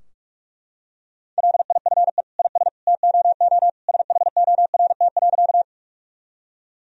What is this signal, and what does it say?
Morse code, heard as a keyed tone, about 42 wpm, 700 Hz. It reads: LIFE IS TOO SHORT.
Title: Sentences from Top 1000 Words - Review Part 2 - 45wpm